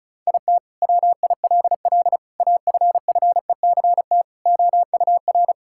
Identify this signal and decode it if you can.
Morse code, heard as a keyed tone, about 36 wpm, 700 Hz. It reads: IT WILL AFFECT OUR
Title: Sentences from Top 1000 Words - Review Part 6 - 35wpm